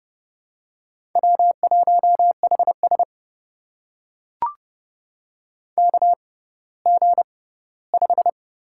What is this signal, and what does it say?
Morse code, heard as a keyed tone, about 30 wpm, spaced as for 17 wpm, 700 Hz. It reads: W1HS KG5
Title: USA Call Signs - 6x Character Spacing - 30wpm